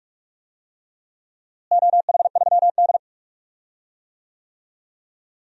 Morse code, heard as a keyed tone, about 41 wpm, 700 Hz. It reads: OH3D
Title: DXCC-100 Call Signs - 4 Characters - Repeated 3x - 45wpm